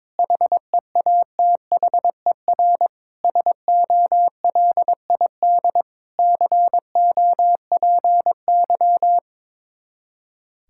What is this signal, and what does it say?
Morse code, heard as a keyed tone, about 22 wpm, 700 Hz. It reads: HEATHER SOLID COPY
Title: K1USN SST QSO Practice 22wpm